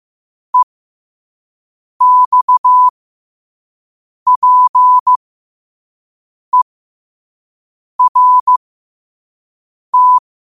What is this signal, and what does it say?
Morse code, heard as a keyed tone, about 15 wpm, spaced as for 5 wpm, 1000 Hz. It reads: EXPERT
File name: qotdfiles / 5wpm